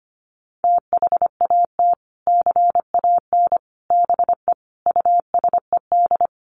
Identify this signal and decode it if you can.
Morse code, heard as a keyed tone, about 25 wpm, 700 Hz. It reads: THAT CAN BE USED
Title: Sets of 5 Words Encore 25wpm